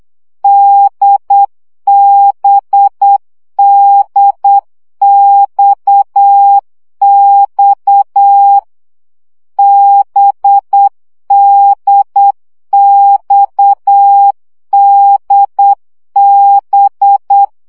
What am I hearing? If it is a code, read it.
Morse code, heard as a keyed tone, about 8 wpm, 800 Hz. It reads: DBDXX BDXDB
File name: CW gyakorlas d b x 7 baud